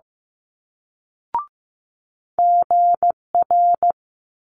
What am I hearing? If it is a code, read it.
Morse code, heard as a keyed tone, about 15 wpm, 700 Hz. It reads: GR
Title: Multi-letter Phonemes Quick Fire 15wpm